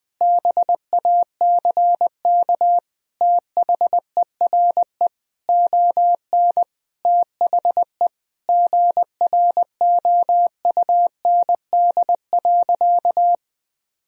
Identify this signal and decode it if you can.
Morse code, heard as a keyed tone, about 20 wpm, 700 Hz. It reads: BACK THERE ON THE GROUND.
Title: Sentences from Top 1000 Words - Review Part 2 - 20wpm